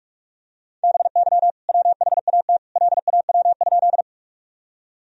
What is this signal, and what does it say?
Morse code, heard as a keyed tone, about 41 wpm, 700 Hz. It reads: BY WHAT LAW?
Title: Sentences from Top 1000 Words - Review Part 3 - 45wpm